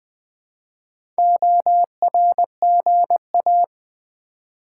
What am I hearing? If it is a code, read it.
Morse code, heard as a keyed tone, about 20 wpm, 700 Hz. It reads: ORGA
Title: Build A Word - 1 Letter At A Time - 5-Letter Words - 20wpm